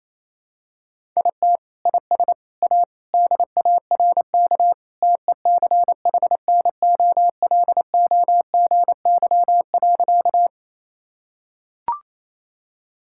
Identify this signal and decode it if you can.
Morse code, heard as a keyed tone, about 28 wpm, 700 Hz. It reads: IT IS A DARK TECHNOLOGY.
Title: Sentences from Top 1000 Words - Review Part 5 - 28wpm